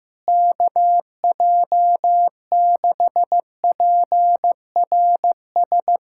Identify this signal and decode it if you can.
Morse code, heard as a keyed tone, about 16 wpm, 700 Hz. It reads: KJ6PRS
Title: USA Callsign DE Callsign 15wpm